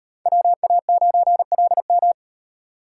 Morse code, heard as a keyed tone, about 35 wpm, 700 Hz. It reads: WA9LM